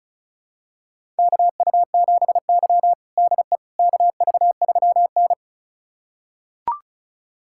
Morse code, heard as a keyed tone, about 34 wpm, 700 Hz. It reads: KU7Y DE KV3N
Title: USA Callsign DE Callsign - 4 Characters - 35wpm